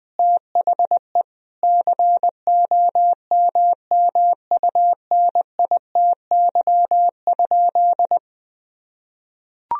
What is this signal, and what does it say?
Morse code, heard as a keyed tone, about 20 wpm, 700 Hz. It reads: THE COMMUNITY?
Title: Sentences from Top 1000 Words - Review Part 4 - 20wpm